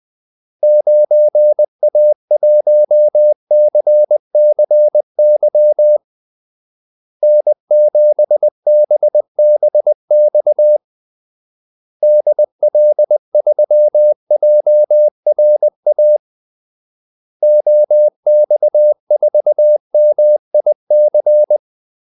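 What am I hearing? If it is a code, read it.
Morse code, heard as a keyed tone, about 20 wpm, 600 Hz. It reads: 9A1CCY N7BBX DL3JRA OX4MIC
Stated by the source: Morse call-sign practice at 20 wpm.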